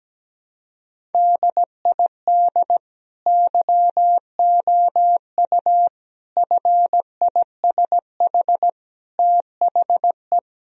Morse code, heard as a keyed tone, about 17 wpm, 700 Hz. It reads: DID YOU FISH THE
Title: Sentences from Top 1000 Words - Review Part 1 - 17wpm